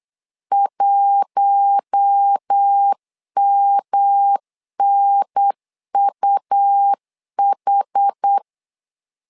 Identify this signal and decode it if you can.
Morse code, heard as a keyed tone, about 8 wpm, 800 Hz. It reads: 1MNUH
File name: LEZIONI / 4 Q1 8wpm 3035